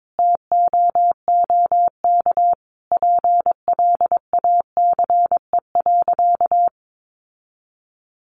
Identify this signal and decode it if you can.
Morse code, heard as a keyed tone, about 22 wpm, 700 Hz. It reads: TOOK PLACE.
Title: Sentences from Top 1500 Words 22wpm